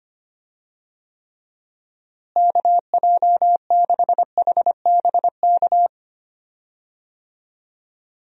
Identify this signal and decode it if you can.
Morse code, heard as a keyed tone, about 25 wpm, 700 Hz. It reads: KJ6HBK
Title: USA Call Signs - 6 Characters - Repeated 3x - 25wpm